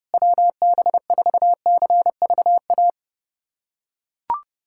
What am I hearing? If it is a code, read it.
Morse code, heard as a keyed tone, about 32 wpm, 700 Hz. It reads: WB4CVA
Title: USA Callsign DE Callsign 30wpm